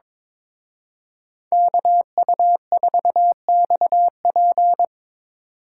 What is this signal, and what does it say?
Morse code, heard as a keyed tone, about 25 wpm, 700 Hz. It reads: KU4XP
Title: Speed Racing - USA Call Signs 22wpm